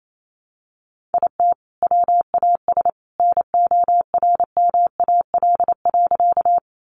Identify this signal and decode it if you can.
Morse code, heard as a keyed tone, about 28 wpm, 700 Hz. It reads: IT WAS NORMAL.
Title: Sentences from Top 1200 Words 28wpm